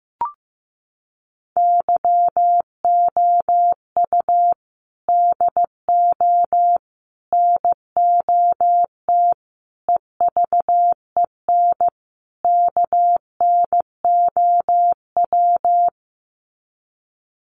Morse code, heard as a keyed tone, about 15 wpm, 700 Hz. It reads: YOU DO NOT EVEN KNOW
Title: Sets of 5 Words Encore2 15wpm